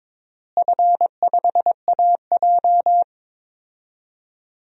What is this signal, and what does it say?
Morse code, heard as a keyed tone, about 22 wpm, 700 Hz. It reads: F5AJ
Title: DXCC-100 Call Signs - 4 Characters - Repeated 3x - 22wpm